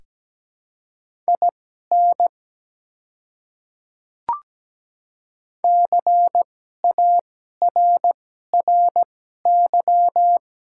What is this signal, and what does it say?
Morse code, heard as a keyed tone, about 17 wpm, spaced as for 12 wpm, 700 Hz. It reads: IN CARRY
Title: 2x Character Spacing - Top 300 Words - 17wpm